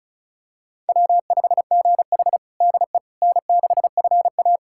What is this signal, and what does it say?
Morse code, heard as a keyed tone, about 36 wpm, 700 Hz. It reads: W5GH DE N6FA
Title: USA Callsign DE Callsign - 4 Characters - 35wpm